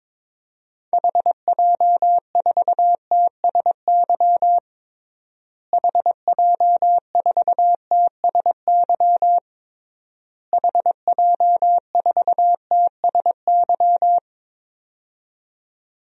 Morse code, heard as a keyed tone, about 21 wpm, 700 Hz. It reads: HJ4TSY HJ4TSY HJ4TSY
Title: DXCC-100 Call Signs - 6 Characters - Repeated 3x - 22wpm